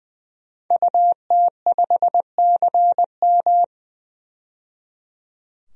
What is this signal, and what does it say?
Morse code, heard as a keyed tone, about 21 wpm, 700 Hz. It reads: UT5CM
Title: DXCC-100 Call Signs - 5 Characters - 20wpm